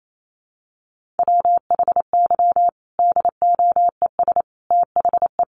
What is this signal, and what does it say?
Morse code, heard as a keyed tone, about 28 wpm, 700 Hz. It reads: WHY DOES THE
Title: Sentences from Top 500 Words 28wpm